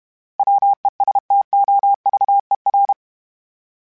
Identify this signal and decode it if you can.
Morse code, heard as a keyed tone, about 32 wpm, 800 Hz. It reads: WESTOVER
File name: fromSMs / FromSMs mp3s / Cities@30